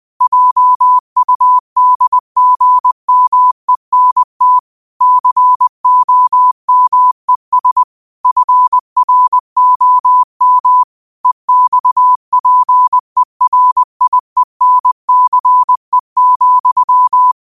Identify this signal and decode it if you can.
Morse code, heard as a keyed tone, about 20 wpm, 1000 Hz. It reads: JUDGMENT COMES FROM EXPERIENCE,
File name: qotdfiles / 20wpm